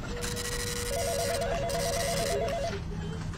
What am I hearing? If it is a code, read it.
There is no code.